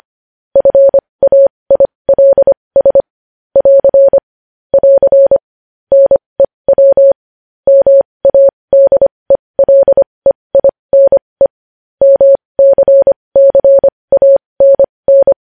Morse code, heard as a keyed tone, about 25 wpm, spaced as for 20 wpm, 550 Hz. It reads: FAILS <AR> <AR> NEW MADELEINE MCCANN
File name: podcast / News headlines-1x-25x20